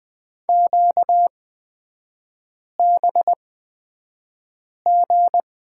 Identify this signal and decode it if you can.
Morse code, heard as a keyed tone, about 20 wpm, spaced as for 5 wpm, 700 Hz. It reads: QBG
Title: 236 Lesson 38e - Letter q - Three Characters 20wpm